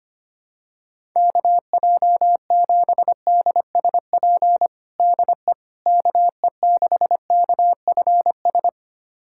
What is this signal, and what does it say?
Morse code, heard as a keyed tone, about 25 wpm, 700 Hz. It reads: KJ7DSP DE KE6KFS
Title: USA Callsign DE Callsign - 6 Characters - 25wpm